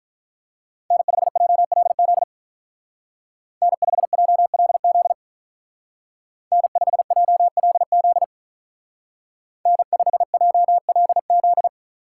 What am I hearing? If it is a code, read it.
Morse code, heard as a keyed tone, about 44 wpm, 700 Hz. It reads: N5JLZ N5JLZ N5JLZ N5JLZ